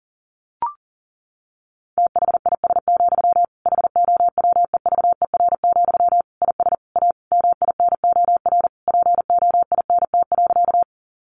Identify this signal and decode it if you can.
Morse code, heard as a keyed tone, about 40 wpm, 700 Hz. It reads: THIS, HOWEVER, IS A MINOR POINT.
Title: Sentences from Top 1500 Words 40wpm